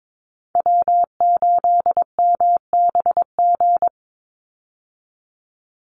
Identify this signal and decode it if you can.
Morse code, heard as a keyed tone, about 22 wpm, 700 Hz. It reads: W8MBG
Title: USA Call Signs 22wpm